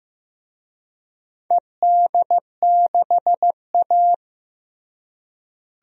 Morse code, heard as a keyed tone, about 15 wpm, 700 Hz. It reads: ED6A
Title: DXCC-100 Call Signs - 4 Characters - Repeated 3x - 15wpm